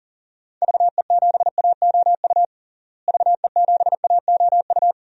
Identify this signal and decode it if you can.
Morse code, heard as a keyed tone, about 40 wpm, 700 Hz. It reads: VE7AOU VE7AOU
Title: DXCC-100 Call Signs - 6 Characters - Repeated 2x - 40wpm